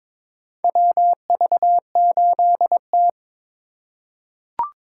Morse code, heard as a keyed tone, about 22 wpm, 700 Hz. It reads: WV8T
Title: USA Call Signs - 4 Characters - 4x Character Spacing - 22wpm